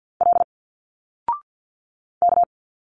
Morse code, heard as a keyed tone, about 50 wpm, 700 Hz. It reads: L K
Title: Single Letters 50wpm